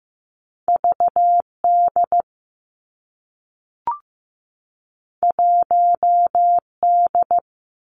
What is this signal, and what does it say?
Morse code, heard as a keyed tone, about 15 wpm, 700 Hz. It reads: V D 1 D
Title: Two Challenging Letter-Number Combinations 15wpm